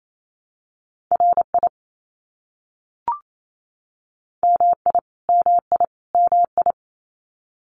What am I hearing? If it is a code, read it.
Morse code, heard as a keyed tone, about 28 wpm, 700 Hz. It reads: RI MI MI MI
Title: US States Repeated 3x 28wpm